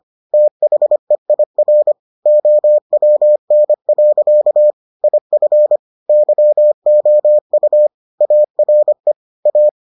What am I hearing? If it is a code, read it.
Morse code, heard as a keyed tone, about 25 wpm, 600 Hz. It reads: THEIR OWN. IF YOU ARE A